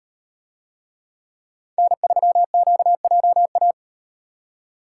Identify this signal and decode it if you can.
Morse code, heard as a keyed tone, about 38 wpm, 700 Hz. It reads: N3QJA